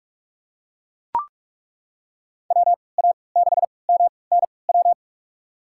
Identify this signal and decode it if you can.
Morse code, heard as a keyed tone, about 41 wpm, spaced as for 25 wpm, 700 Hz. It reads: WA6MNW
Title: USA Call Signs - 6 Characters - 3x Character Spacing - 45wpm